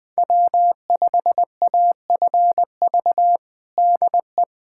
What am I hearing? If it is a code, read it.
Morse code, heard as a keyed tone, about 20 wpm, 700 Hz. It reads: W5AFV DE
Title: USA Callsign DE Callsign 20wpm